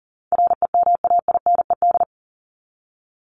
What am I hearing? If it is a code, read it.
Morse code, heard as a keyed tone, about 40 wpm, 700 Hz. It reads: REMAINED
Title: ED Suffix - Top 100 Words with Base Words 40wpm